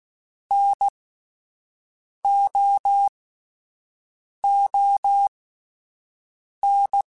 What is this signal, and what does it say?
Morse code, heard as a keyed tone, about 16 wpm, spaced as for 5 wpm, 800 Hz. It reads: NOON